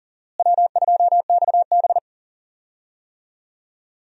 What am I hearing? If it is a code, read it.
Morse code, heard as a keyed tone, about 38 wpm, 700 Hz. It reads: W2XB
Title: USA Call Signs - 4 Characters - Repeated 3x - 40wpm